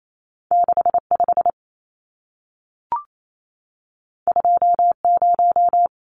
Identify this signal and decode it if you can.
Morse code, heard as a keyed tone, about 28 wpm, 700 Hz. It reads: 65 20
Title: Numbers - 2-Digit 28wpm